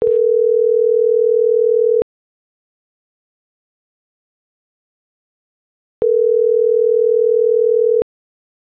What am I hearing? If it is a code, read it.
Ringback tone (North America), 440 + 480 Hz.